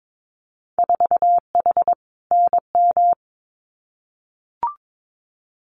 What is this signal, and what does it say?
Morse code, heard as a keyed tone, about 22 wpm, 700 Hz. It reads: 4H NM
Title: Winter Field Day Exchanges Encore 22wpm